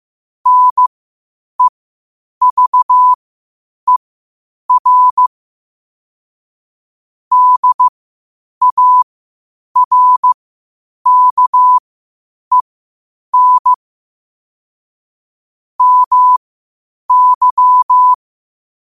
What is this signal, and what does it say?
Morse code, heard as a keyed tone, about 15 wpm, spaced as for 8 wpm, 1000 Hz. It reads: NEVER DARKEN MY
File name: qotdfiles / 8wpm